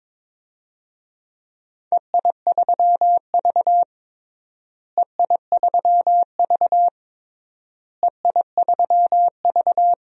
Morse code, heard as a keyed tone, about 21 wpm, 700 Hz. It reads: EI3V EI3V EI3V
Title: DXCC-100 Call Signs - 4 Characters - Repeated 3x - 22wpm